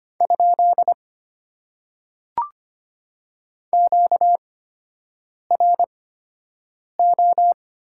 Morse code, heard as a keyed tone, about 25 wpm, spaced as for 7 wpm, 690 Hz. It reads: ? QRO
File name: Top 90 QSO Elements - 8x Character Spacing - 25wpm